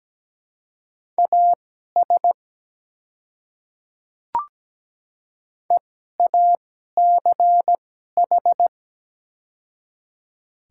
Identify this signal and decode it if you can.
Morse code, heard as a keyed tone, about 17 wpm, spaced as for 12 wpm, 700 Hz. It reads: AS EACH
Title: 2x Character Spacing - Top 200 Words - 17wpm